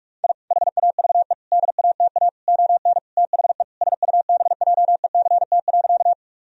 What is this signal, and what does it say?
Morse code, heard as a keyed tone, about 46 wpm, 700 Hz. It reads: I HAVE DATA ON THE SUBJECT.
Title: Sentences from Top 1000 Words - Review Part 3 - 45wpm